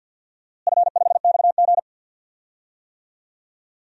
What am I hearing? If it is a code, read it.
Morse code, heard as a keyed tone, about 52 wpm, 700 Hz. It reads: F5XG
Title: DXCC-100 Call Signs - 4 Characters - Repeated 2x - 50wpm